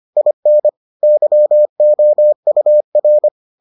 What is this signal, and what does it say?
Morse code, heard as a keyed tone, about 25 wpm, 600 Hz. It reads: IN YOUR